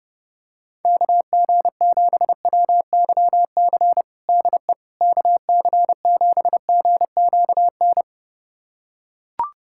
Morse code, heard as a keyed tone, about 30 wpm, 700 Hz. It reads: KG7WYC DE KC7GQN
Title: USA Callsign DE Callsign 30wpm